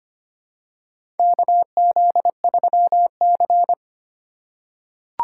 Morse code, heard as a keyed tone, about 25 wpm, 700 Hz. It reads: KZ3C
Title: USA Call Signs - 4 Characters - 4x Character Spacing - 25wpm